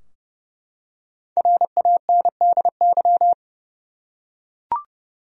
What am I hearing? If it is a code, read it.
Morse code, heard as a keyed tone, about 30 wpm, 700 Hz. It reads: RANDY
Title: USA Common Names - Max 5 Letters - 30wpm